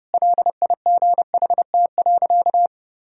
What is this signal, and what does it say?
Morse code, heard as a keyed tone, about 30 wpm, 700 Hz. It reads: LIGHT.
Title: Sentences from Top 1000 Words - Review Part 1 - 30wpm